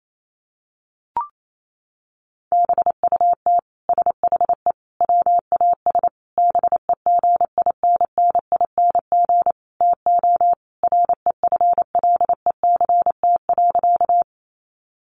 Morse code, heard as a keyed tone, about 28 wpm, 700 Hz. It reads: BUT SHE WAS BEGINNING TO REFLECT.
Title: Sentences from Top 1500 Words 28wpm